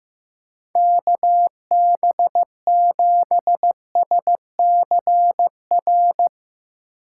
Morse code, heard as a keyed tone, about 15 wpm, 700 Hz. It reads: KB7SCR
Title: USA Call Signs - 3x Character Spacing - 15wpm